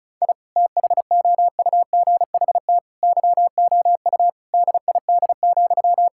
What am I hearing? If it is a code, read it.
Morse code, heard as a keyed tone, about 36 wpm, 700 Hz. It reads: I THOUGHT YOU DID,
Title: Sentences from Top 1000 Words - Review Part 5 - 35wpm